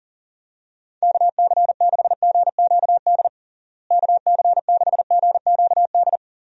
Morse code, heard as a keyed tone, about 40 wpm, 700 Hz. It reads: KC6GQD KC6GQD
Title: USA Call Signs - 6 Characters - Repeated 2x - 40wpm